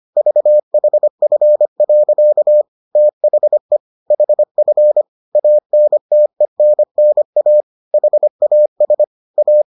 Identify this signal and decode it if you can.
Morse code, heard as a keyed tone, about 25 wpm, 600 Hz. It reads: VHF. THE HF ANTENNA HAS A